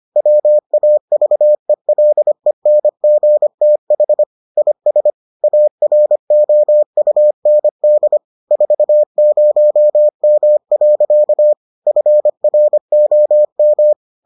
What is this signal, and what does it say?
Morse code, heard as a keyed tone, about 25 wpm, 600 Hz. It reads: WAVELENGTH IS AROUND 40M. FROM